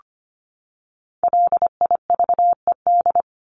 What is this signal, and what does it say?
Morse code, heard as a keyed tone, about 25 wpm, 700 Hz. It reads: LIVED
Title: ED Suffix - Top 100 Words with Base Words 25wpm